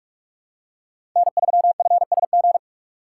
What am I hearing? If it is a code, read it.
Morse code, heard as a keyed tone, about 45 wpm, 700 Hz. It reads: N3FSG
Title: USA Call Signs - 5 Characters - 2x Character Spacing - 45wpm